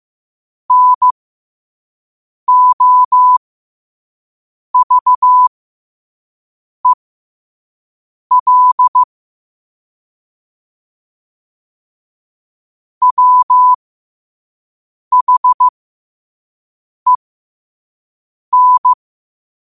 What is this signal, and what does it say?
Morse code, heard as a keyed tone, about 15 wpm, spaced as for 5 wpm, 1000 Hz. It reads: NOVEL WHEN
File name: qotdfiles / 5wpm